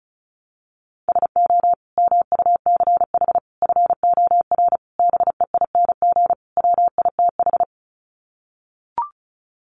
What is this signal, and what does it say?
Morse code, heard as a keyed tone, about 34 wpm, 700 Hz. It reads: SO MUCH FOR BEING WITH
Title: Sets of 5 Words Encore 35wpm